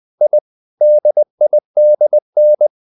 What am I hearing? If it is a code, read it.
Morse code, heard as a keyed tone, about 20 wpm, 600 Hz. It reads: I DIDN
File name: the old man and the sea part 2 20wpm